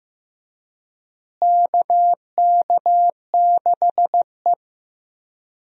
Morse code, heard as a keyed tone, about 14 wpm, 700 Hz. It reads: KK6E
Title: USA Call Signs - 4 Characters - Repeated 3x - 15wpm